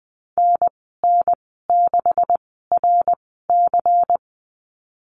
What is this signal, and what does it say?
Morse code, heard as a keyed tone, about 20 wpm, 700 Hz. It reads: N N 6 R C
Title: 154 Lesson 25g - Letter p - Call Signs 20wpm